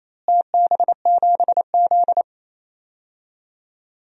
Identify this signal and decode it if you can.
Morse code, heard as a keyed tone, about 28 wpm, 700 Hz. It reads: TB7Z
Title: DXCC-100 Call Signs - 4 Characters - Repeated 2x - 28wpm